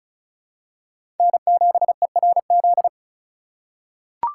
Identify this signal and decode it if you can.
Morse code, heard as a keyed tone, about 36 wpm, 700 Hz. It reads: N7ERZ
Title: USA Call Signs - 5 Characters - 2x Character Spacing - 35wpm